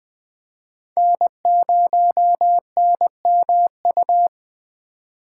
Morse code, heard as a keyed tone, about 20 wpm, 700 Hz. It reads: N0NMU
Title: USA Call Signs - 5 Characters - 4x Character Spacing - 20wpm